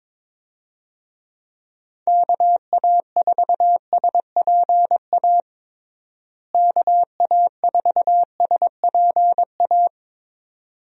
Morse code, heard as a keyed tone, about 22 wpm, 700 Hz. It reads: KA4SPA KA4SPA
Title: USA Call Signs - 6 Characters - Repeated 2x - 22wpm